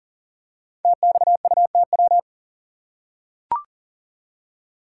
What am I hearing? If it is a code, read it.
Morse code, heard as a keyed tone, about 39 wpm, 700 Hz. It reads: TXUTW
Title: Code Groups - 5 Letters - 40wpm